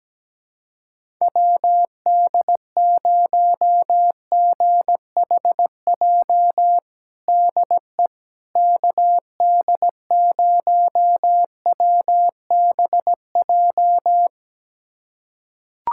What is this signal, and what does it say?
Morse code, heard as a keyed tone, about 17 wpm, 700 Hz. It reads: WD0GHJ DE KD0WBJ
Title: USA Callsign DE Callsign - 6 Characters - 17wpm